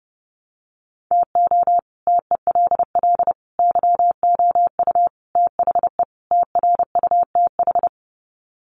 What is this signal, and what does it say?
Morse code, heard as a keyed tone, about 30 wpm, 700 Hz. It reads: TO TELL YOU THE TRUTH
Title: Sets of 5 Words 30wpm